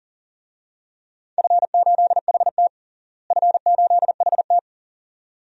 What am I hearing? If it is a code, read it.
Morse code, heard as a keyed tone, about 40 wpm, 700 Hz. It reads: F8HT F8HT
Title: DXCC-100 Call Signs - 4 Characters - Repeated 2x - 40wpm